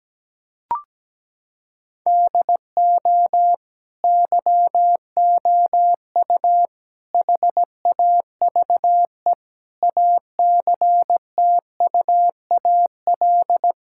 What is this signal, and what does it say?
Morse code, heard as a keyed tone, about 17 wpm, 700 Hz. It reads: DO YOU HAVE ACTUAL
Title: Sentences from Top 1000 Words - Review Part 5 - 17wpm